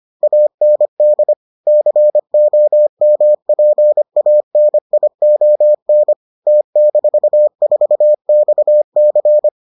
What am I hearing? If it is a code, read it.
Morse code, heard as a keyed tone, about 25 wpm, 600 Hz. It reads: AND COMPANION T-4XC